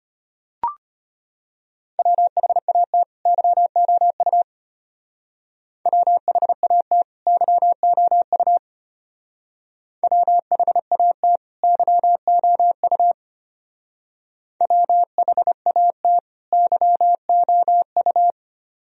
Morse code, heard as a keyed tone, about 32 wpm, 700 Hz. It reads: WHAT YOU WHAT YOU WHAT YOU WHAT YOU